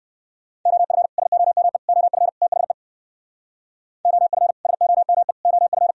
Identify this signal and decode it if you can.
Morse code, heard as a keyed tone, about 66 wpm, 700 Hz. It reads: OF SOME OF THE OF SOME OF